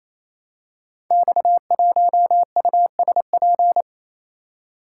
Morse code, heard as a keyed tone, about 28 wpm, 700 Hz. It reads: X1USP
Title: Code Groups - 5 Letter:Numbers - 28wpm